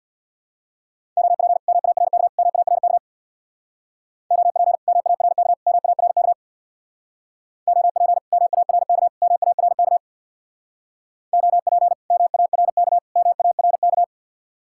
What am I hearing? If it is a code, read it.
Morse code, heard as a keyed tone, about 61 wpm, 700 Hz. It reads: OP MARK MARK OP MARK MARK OP MARK MARK OP MARK MARK
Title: Speed-Racing Common QSO Phrases Encore - 50wpm